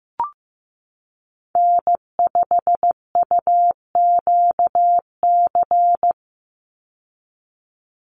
Morse code, heard as a keyed tone, about 15 wpm, 700 Hz. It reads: N5UQC
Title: USA Call Signs 15wpm